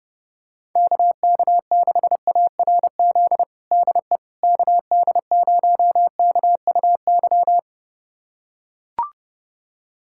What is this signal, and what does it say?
Morse code, heard as a keyed tone, about 31 wpm, 700 Hz. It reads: KK6ARZ DE KD0KUY
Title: USA Callsign DE Callsign 30wpm